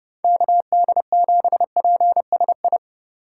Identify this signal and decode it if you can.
Morse code, heard as a keyed tone, about 30 wpm, 700 Hz. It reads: KD7PSI
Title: USA Call Signs - 6 Characters - 2x Character Spacing - 30wpm